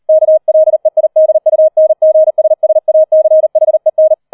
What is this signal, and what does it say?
Morse code, heard as a keyed tone, about 38 wpm, 630 Hz. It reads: KLEIDUNGSSACHEN
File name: 003 Dritter Streich 40wpm